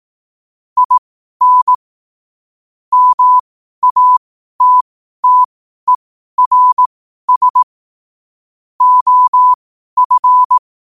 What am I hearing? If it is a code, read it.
Morse code, heard as a keyed tone, about 18 wpm, spaced as for 12 wpm, 1000 Hz. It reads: IN MATTERS OF
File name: qotdfiles / 12wpm